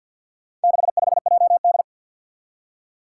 Morse code, heard as a keyed tone, about 48 wpm, 700 Hz. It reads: 65JD
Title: Four Challenging Letter-Number Combinations 50wpm